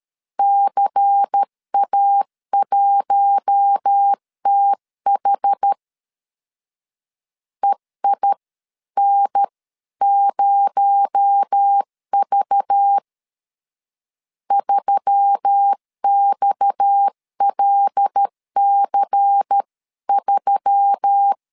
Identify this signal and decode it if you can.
Morse code, heard as a keyed tone, about 13 wpm, 800 Hz. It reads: CA1TH EIN0V 3XLC3
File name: CW / DET / 1 6 1500C S C 12wpm 3533